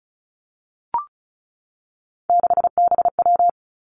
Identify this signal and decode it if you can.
Morse code, heard as a keyed tone, about 36 wpm, 700 Hz. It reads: 6BW
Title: Three Challenging Letter-Number Combinations 35wpm